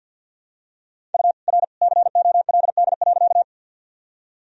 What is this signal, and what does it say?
Morse code, heard as a keyed tone, about 47 wpm, 700 Hz. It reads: U R COLD.
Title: Sentences from Top 400 Words and Abbreviations 50wpm